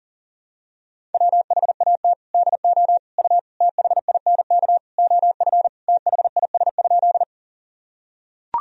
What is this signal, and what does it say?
Morse code, heard as a keyed tone, about 40 wpm, 700 Hz. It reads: WHAT DO U THINK OF THIS?
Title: Sentences from Top 200 Words and Abbreviations 40wpm